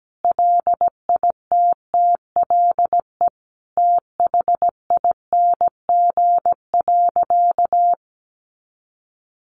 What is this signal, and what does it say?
Morse code, heard as a keyed tone, about 17 wpm, 700 Hz. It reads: LITTLE THING.
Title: Sentences from Top 1000 Words 17wpm